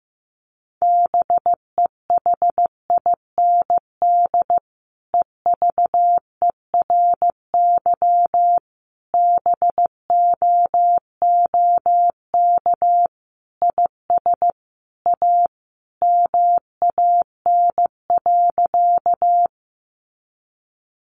Morse code, heard as a keyed tone, about 14 wpm, 700 Hz. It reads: BEHIND EVERY BOOK IS A MAN.
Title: Sentences from Top 500 Words 15wpm